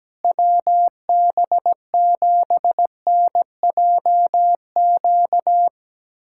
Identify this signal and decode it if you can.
Morse code, heard as a keyed tone, about 17 wpm, 700 Hz. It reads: WB7NJQ
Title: USA Callsign DE Callsign - 6 Characters - 17wpm